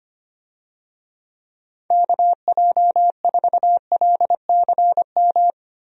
Morse code, heard as a keyed tone, about 25 wpm, 700 Hz. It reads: KJ4LCM
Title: USA Call Signs - 6 Characters - 25wpm